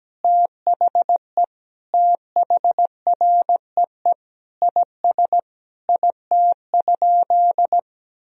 Morse code, heard as a keyed tone, about 17 wpm, 700 Hz. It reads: THE THREE IS IT?
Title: Sentences from Top 1000 Words - Review Part 1 - 17wpm